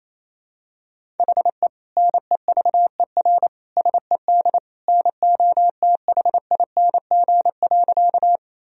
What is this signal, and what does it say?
Morse code, heard as a keyed tone, about 28 wpm, 700 Hz. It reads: HE NEVER SED NOTHING.
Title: Sentences from Top 500 Words and Abbreviations 28wpm